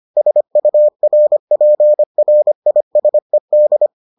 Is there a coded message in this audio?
Yes: Morse code from 0.2 to 3.9 s, heard as a keyed tone, about 25 wpm, 600 Hz. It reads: SURPRISED